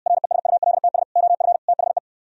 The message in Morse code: RESULTS OF THE

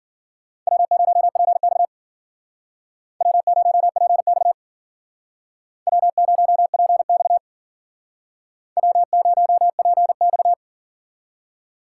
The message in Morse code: W0PX W0PX W0PX W0PX

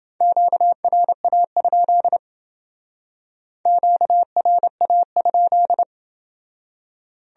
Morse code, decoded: QRA? QRA?

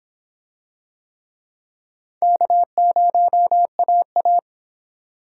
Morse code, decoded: K0AA

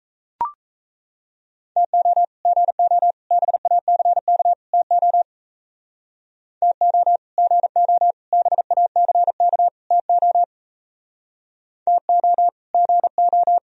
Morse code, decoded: TO GO BACK TO TO GO BACK TO TO GO

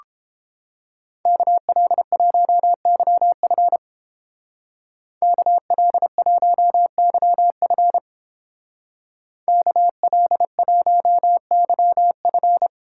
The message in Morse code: KL1YF KL1YF KL1YF

